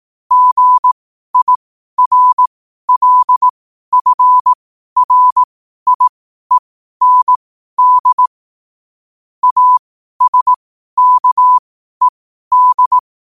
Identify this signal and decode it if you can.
Morse code, heard as a keyed tone, about 18 wpm, spaced as for 12 wpm, 1000 Hz. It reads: GIRLFRIEND ASKED
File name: qotdfiles / 12wpm